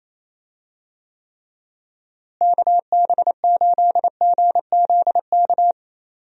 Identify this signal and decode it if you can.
Morse code, heard as a keyed tone, about 27 wpm, 700 Hz. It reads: KB8GZK
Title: USA Call Signs - 6 Characters - Repeated 2x - 28wpm